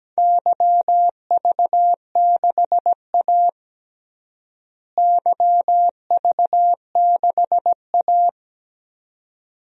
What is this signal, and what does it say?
Morse code, heard as a keyed tone, about 17 wpm, 700 Hz. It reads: YV6A YV6A
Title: DXCC-100 Call Signs - 4 Characters - Repeated 2x - 17wpm